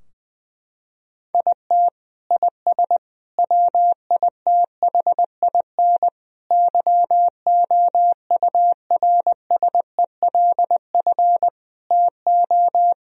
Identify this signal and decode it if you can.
Morse code, heard as a keyed tone, about 20 wpm, 700 Hz. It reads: IT IS WITHIN YOURSELF TO
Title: Sentences from Top 1000 Words - Review Part 4 - 20wpm